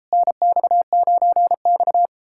Morse code, heard as a keyed tone, about 38 wpm, 700 Hz. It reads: NX9X